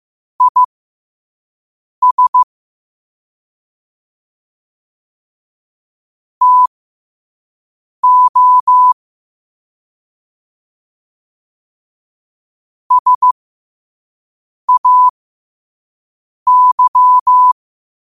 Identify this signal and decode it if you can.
Morse code, heard as a keyed tone, about 15 wpm, spaced as for 5 wpm, 1000 Hz. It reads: IS TO SAY
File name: qotdfiles / 5wpm